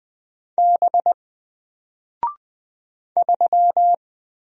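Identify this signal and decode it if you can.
Morse code, heard as a keyed tone, about 20 wpm, 700 Hz. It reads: B 3
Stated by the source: Single letter and number Morse drill at 20 wpm, from N4LCD.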